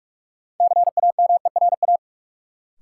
Morse code, heard as a keyed tone, about 43 wpm, 700 Hz. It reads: CAMERA
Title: Sentences from Top 1000 Words - Review Part 5 - New Word Drill - 45wpm